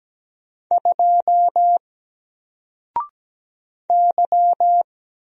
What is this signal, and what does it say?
Morse code, heard as a keyed tone, about 17 wpm, 700 Hz. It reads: 2 Y
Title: Single Letter-Number 17wpm